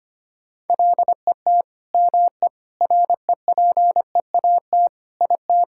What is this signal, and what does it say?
Morse code, heard as a keyed tone, about 25 wpm, 700 Hz. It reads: LET ME REPEAT IT
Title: Sentences from Top 1600 Words 25wpm